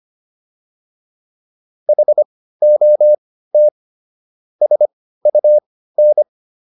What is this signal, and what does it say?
Morse code, heard as a keyed tone, about 25 wpm, spaced as for 15 wpm, 600 Hz. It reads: HOT SUN